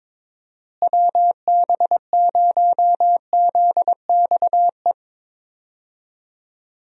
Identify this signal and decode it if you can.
Morse code, heard as a keyed tone, about 22 wpm, 700 Hz. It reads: WB0ZXE
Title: USA Call Signs - 6 Characters - Repeated 3x - 22wpm